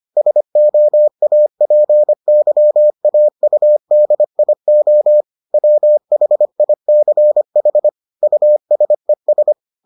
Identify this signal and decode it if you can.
Morse code, heard as a keyed tone, about 25 wpm, 600 Hz. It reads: SOAPYAUDIO WHICH USES